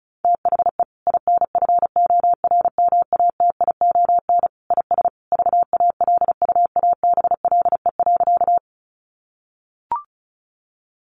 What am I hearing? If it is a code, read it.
Morse code, heard as a keyed tone, about 34 wpm, 700 Hz. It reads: THE INFORMATION IS VALUABLE.
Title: Sentences from Top 1100 Words 35wpm